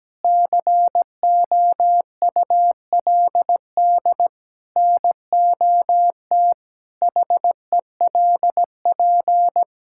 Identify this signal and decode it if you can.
Morse code, heard as a keyed tone, about 17 wpm, 700 Hz. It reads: COULD NOT HELP
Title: Sentences from Top 1000 Words - Review Part 5 - 17wpm